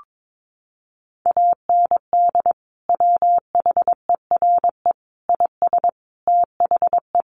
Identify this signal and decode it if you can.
Morse code, heard as a keyed tone, about 22 wpm, 700 Hz. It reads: AND WHERE IS THE
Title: Sentences from Top 1100 Words 22wpm